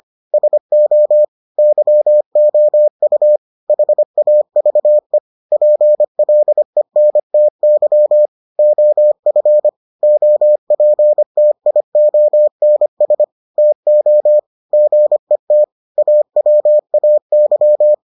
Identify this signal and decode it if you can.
Morse code, heard as a keyed tone, about 25 wpm, 600 Hz. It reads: SO YOU HAVE PLENTY OF OPTIONS TO GET AWAY